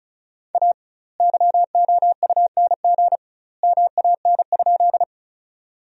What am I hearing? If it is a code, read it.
Morse code, heard as a keyed tone, about 34 wpm, 700 Hz. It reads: A YOUNG MAN?
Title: 2x Word-Spacing - Sentences from Top 400 Words - 35wpm